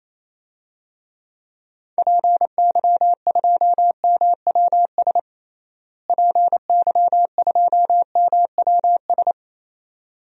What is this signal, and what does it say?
Morse code, heard as a keyed tone, about 27 wpm, 700 Hz. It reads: PY2MWS PY2MWS
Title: DXCC-100 Call Signs - 6 Characters - Repeated 2x - 28wpm